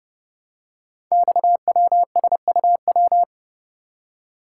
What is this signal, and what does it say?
Morse code, heard as a keyed tone, about 30 wpm, 700 Hz. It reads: XWSUW
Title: Code Groups - 5 Letters - 30wpm